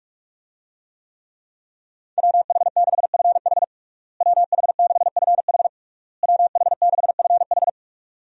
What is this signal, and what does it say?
Morse code, heard as a keyed tone, about 46 wpm, 700 Hz. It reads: WH6FH WH6FH WH6FH
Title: USA Call Signs - 5 Characters - Repeated 3x - 45wpm